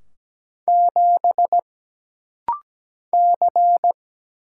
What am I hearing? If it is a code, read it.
Morse code, heard as a keyed tone, about 17 wpm, 700 Hz. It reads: 7 C
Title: Single Letter-Number - Rapid-Fire 17wpm